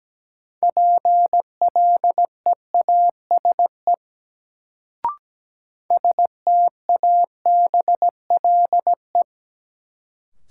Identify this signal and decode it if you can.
Morse code, heard as a keyed tone, about 17 wpm, 700 Hz. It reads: PLEASE STABLE
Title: Sentences from Top 1500 Words - New Word Drill - 17wpm